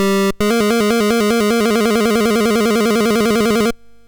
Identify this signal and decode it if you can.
There is no code.